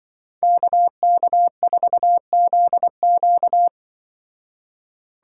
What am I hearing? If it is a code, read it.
Morse code, heard as a keyed tone, about 26 wpm, 700 Hz. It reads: KK4ZQ